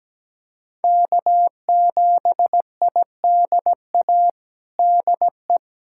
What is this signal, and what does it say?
Morse code, heard as a keyed tone, about 17 wpm, 700 Hz. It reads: K7IDA DE
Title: USA Callsign DE Callsign - 5 Characters - 17wpm